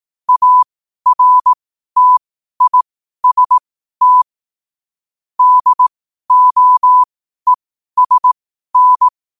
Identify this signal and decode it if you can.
Morse code, heard as a keyed tone, about 18 wpm, spaced as for 12 wpm, 1000 Hz. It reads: ARTIST DOESN
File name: qotdfiles / 12wpm